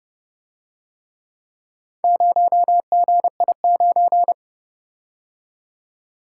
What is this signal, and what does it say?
Morse code, heard as a keyed tone, about 30 wpm, 700 Hz. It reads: 0GI9